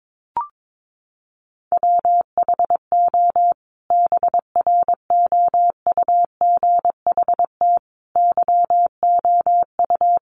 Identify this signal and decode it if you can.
Morse code, heard as a keyed tone, about 22 wpm, 700 Hz. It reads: WHO BROUGHT YOU
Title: Sentences from Top 500 Words 22wpm